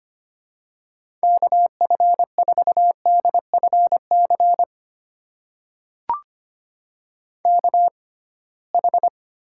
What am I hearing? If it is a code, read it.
Morse code, heard as a keyed tone, about 25 wpm, spaced as for 19 wpm, 700 Hz. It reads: KF4DFC KH
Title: USA Call Signs - 6x Character Spacing - 25wpm